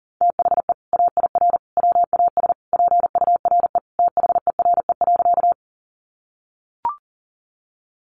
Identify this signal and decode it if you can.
Morse code, heard as a keyed tone, about 40 wpm, 700 Hz. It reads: THE AIR WAS PURE THERE.
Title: Sentences from Top 1500 Words 40wpm